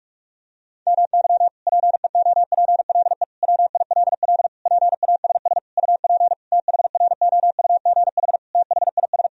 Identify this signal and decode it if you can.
Morse code, heard as a keyed tone, about 46 wpm, 700 Hz. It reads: MY PEOPLE WILL PASS UP THROUGH THIS
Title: Sentences from Top 1000 Words - Review Part 2 - 45wpm